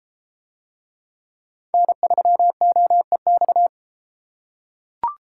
Morse code, heard as a keyed tone, about 32 wpm, 700 Hz. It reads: N3OEX